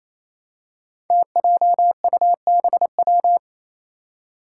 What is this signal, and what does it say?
Morse code, heard as a keyed tone, about 28 wpm, 700 Hz. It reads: TJUBW
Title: Five Challenging Letter-Number Combinations 28wpm